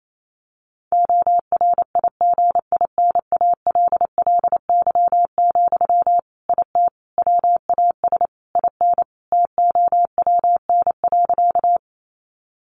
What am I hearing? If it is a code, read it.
Morse code, heard as a keyed tone, about 28 wpm, 700 Hz. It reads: ORIGINALLY, IT WAS IN TOWN.
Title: Sentences from Top 1500 Words 28wpm